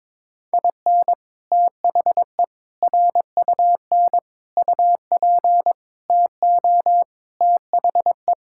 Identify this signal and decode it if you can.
Morse code, heard as a keyed tone, about 21 wpm, 700 Hz. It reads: IN THE RUN UP TO THE
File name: practice / 22wpm / Sets of 6 Words - 22wpm